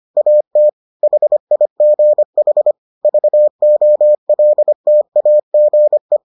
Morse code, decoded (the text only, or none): AT HIGH VOLTAGE